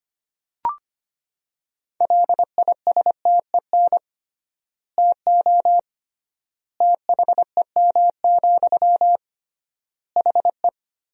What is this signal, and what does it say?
Morse code, heard as a keyed tone, about 25 wpm, 700 Hz. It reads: LISTEN TO THEM, HE
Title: 3x Word-Spacing - Sentences from Top 400 Words - 25wpm